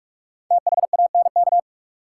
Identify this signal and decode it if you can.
Morse code, heard as a keyed tone, about 43 wpm, 700 Hz. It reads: THANK
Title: Sets of 2 Words - New Word Drill - 45wpm